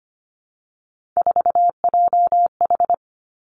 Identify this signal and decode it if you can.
Morse code, heard as a keyed tone, about 25 wpm, 700 Hz. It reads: 4JH
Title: Three Challenging Letter-Number Combinations 25wpm